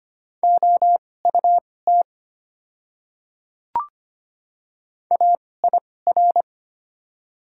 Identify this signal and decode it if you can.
Morse code, heard as a keyed tone, about 25 wpm, spaced as for 18 wpm, 700 Hz. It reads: OUT AIR
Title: 2x Character Spacing - Top 300 Words - 25wpm